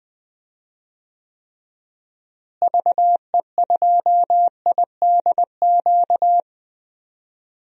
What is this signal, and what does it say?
Morse code, heard as a keyed tone, about 20 wpm, 700 Hz. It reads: VE2IDQ
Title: DXCC-100 Call Signs - 6 Characters - Repeated 2x - 20wpm